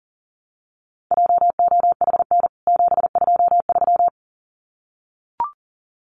Morse code, heard as a keyed tone, about 38 wpm, 700 Hz. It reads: JOHN 723
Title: CWT Exchange 40wpm